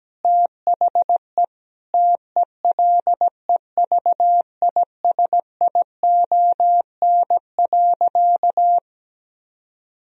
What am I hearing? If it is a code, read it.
Morse code, heard as a keyed tone, about 17 wpm, 700 Hz. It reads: THE TELEVISION.
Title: Sentences from Top 1000 Words - Review Part 4 - 17wpm